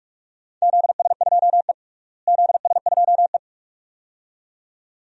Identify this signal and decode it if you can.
Morse code, heard as a keyed tone, about 41 wpm, 700 Hz. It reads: ZS2E ZS2E
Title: DXCC-100 Call Signs - 4 Characters - Repeated 2x - 45wpm